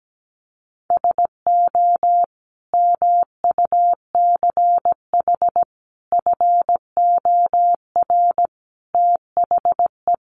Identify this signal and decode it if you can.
Morse code, heard as a keyed tone, about 17 wpm, 700 Hz. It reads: SO MUCH FOR THE
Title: Sentences from Top 1000 Words 17wpm